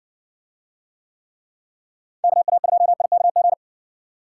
Morse code, heard as a keyed tone, about 60 wpm, 700 Hz. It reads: KA2IKG